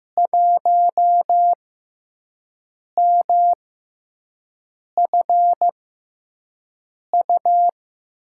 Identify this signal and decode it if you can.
Morse code, heard as a keyed tone, about 15 wpm, spaced as for 5 wpm, 700 Hz. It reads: 1MFU
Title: USA Call Signs - 6 Characters - 6x Character Spacing - 15wpm